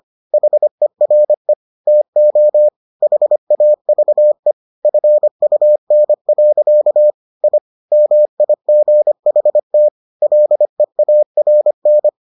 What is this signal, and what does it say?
Morse code, heard as a keyed tone, about 25 wpm, 600 Hz. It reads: HERE TO HAVE FUN. I MIGHT LEARN